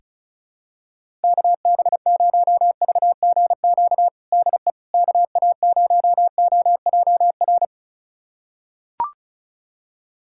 Morse code, heard as a keyed tone, about 34 wpm, 700 Hz. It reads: KB0VGQ DE KA0OJR
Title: USA Callsign DE Callsign 35wpm